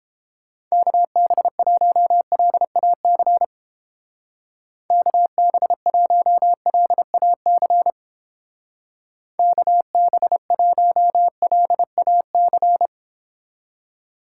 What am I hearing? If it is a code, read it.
Morse code, heard as a keyed tone, about 31 wpm, 700 Hz. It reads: KB1LAC KB1LAC KB1LAC